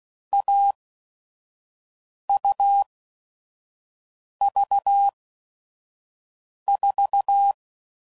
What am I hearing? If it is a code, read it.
Morse code, heard as a keyed tone, about 16 wpm, spaced as for 5 wpm, 800 Hz. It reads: AUV4
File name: Combinations / Difficult Combos / AUV AUV4